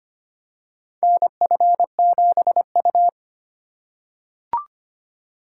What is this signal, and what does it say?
Morse code, heard as a keyed tone, about 25 wpm, 700 Hz. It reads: NF7U
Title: USA Call Signs - 3x Character Spacing - 25wpm